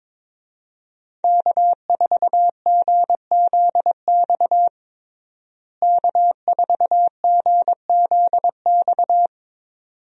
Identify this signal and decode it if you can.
Morse code, heard as a keyed tone, about 21 wpm, 700 Hz. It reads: K4GZX K4GZX
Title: USA Call Signs - 5 Characters - Repeated 2x - 22wpm